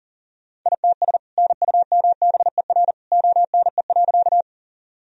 Morse code, heard as a keyed tone, about 40 wpm, 700 Hz. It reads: ITS NUMBER ONE.
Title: Sentences from Top 1000 Words - Review Part 1 - 40wpm